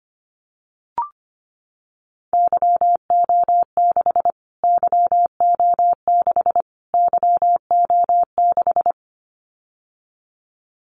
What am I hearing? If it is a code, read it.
Morse code, heard as a keyed tone, about 25 wpm, 700 Hz. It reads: YO6 YO6 YO6
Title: Callsign Prefixes for Top 25 DXCC - Repeated 3x - 25wpm